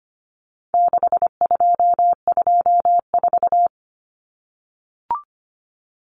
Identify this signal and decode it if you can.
Morse code, heard as a keyed tone, about 25 wpm, 700 Hz. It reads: 6224
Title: Common Cut Numbers - 4-Digit 25wpm